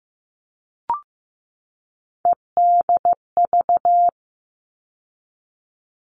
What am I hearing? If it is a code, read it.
Morse code, heard as a keyed tone, about 15 wpm, 700 Hz. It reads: EDV
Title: All Cut Numbers - 3-Digit 15wpm